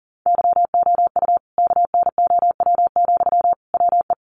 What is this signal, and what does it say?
Morse code, heard as a keyed tone, about 40 wpm, 700 Hz. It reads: YOU KNOW, WE
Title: Sentences from Top 600 Words 40wpm